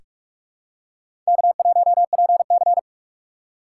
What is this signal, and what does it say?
Morse code, heard as a keyed tone, about 44 wpm, 700 Hz. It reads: K1PC
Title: USA Call Signs - 4 Characters - 4x Character Spacing - 45wpm